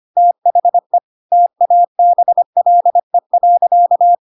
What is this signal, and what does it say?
Morse code, heard as a keyed tone, about 25 wpm, 700 Hz. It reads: THE TABLE.
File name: Pinocchio / Pinocchio ch7 25 25 ik1hge